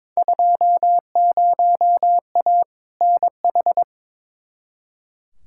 Morse code, heard as a keyed tone, about 21 wpm, 700 Hz. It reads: 20A NH
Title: Field Day Exchanges Encore - 22wpm